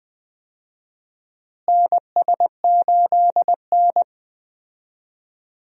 Morse code, heard as a keyed tone, about 20 wpm, 700 Hz. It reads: NS8N
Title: USA Call Signs - 4 Characters - Repeated 3x - 20wpm